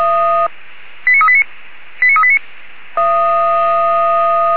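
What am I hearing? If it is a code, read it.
Five-tone selective call (ZVEI), heard as quick runs of five tones, tones of about 70 ms. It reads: 89389 89389
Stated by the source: ZVEI five-tone selective call 89406, received by SDR.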